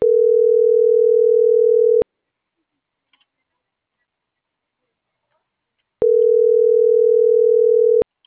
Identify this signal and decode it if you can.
Ringback tone (North America), 440 + 480 Hz.